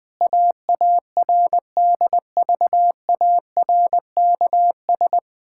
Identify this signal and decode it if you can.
Morse code, heard as a keyed tone, about 21 wpm, 700 Hz. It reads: AARDVARKS
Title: Pangram Challenge v2 - 20wpm